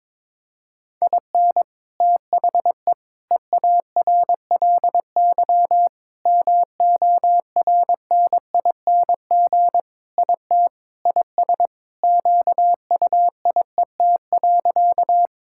Morse code, heard as a keyed tone, about 22 wpm, 700 Hz. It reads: IN THE EARLY MORNING IT IS QUIET.
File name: practice / 22wpm / Sentences with 3-Word Phrases - 22wpm